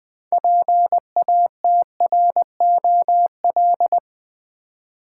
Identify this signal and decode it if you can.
Morse code, heard as a keyed tone, about 20 wpm, 700 Hz. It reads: PATROL
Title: 6 Letter Words - First 3 Letters Given - 20wpm